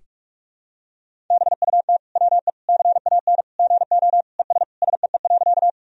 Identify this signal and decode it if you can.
Morse code, heard as a keyed tone, about 46 wpm, 700 Hz. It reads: BUT WE CAN GO ES SEE.
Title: Sentences from Top 100 Words and Abbreviations 45wpm